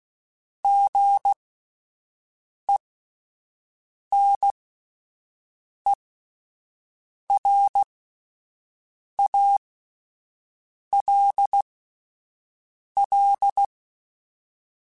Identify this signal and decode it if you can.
Morse code, heard as a keyed tone, about 16 wpm, spaced as for 5 wpm, 800 Hz. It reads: GENERALL
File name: Spaced Characters / Spaced Characters-mp3s / Weather5 12 sp @10